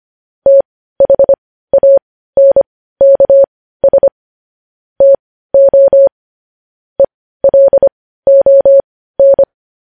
Morse code, heard as a keyed tone, about 25 wpm, spaced as for 15 wpm, 550 Hz. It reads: THANKS TO ELON
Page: News Headlines in Morse Code at 15 WPM